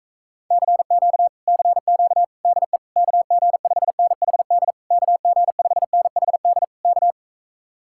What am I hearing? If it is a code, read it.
Morse code, heard as a keyed tone, about 45 wpm, 700 Hz. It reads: CQ CQ DE KG5NHD KG5NHD K